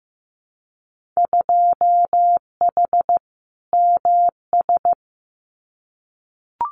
Morse code, heard as a keyed tone, about 15 wpm, 700 Hz. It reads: 2H MS
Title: Winter Field Day Exchanges 15wpm